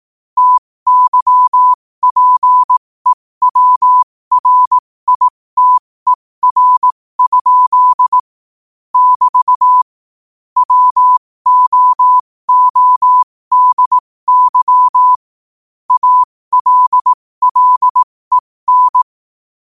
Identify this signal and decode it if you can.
Morse code, heard as a keyed tone, about 18 wpm, spaced as for 15 wpm, 1000 Hz. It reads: TYPEWRITER? <BT> WOODY ALLEN
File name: qotdfiles / 15wpm